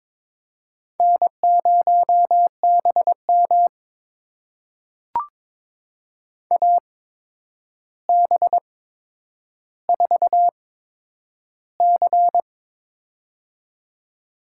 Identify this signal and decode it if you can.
Morse code, heard as a keyed tone, about 22 wpm, spaced as for 9 wpm, 700 Hz. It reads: N0BM AB4C
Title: USA Call Signs - 4 Characters - 8x Character Spacing - 22wpm